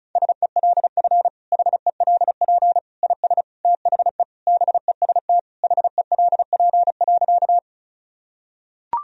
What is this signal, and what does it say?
Morse code, heard as a keyed tone, about 36 wpm, 700 Hz. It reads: SELF HELP IS THE BEST HELP.
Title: Sentences from Top 1000 Words - Review Part 1 - 35wpm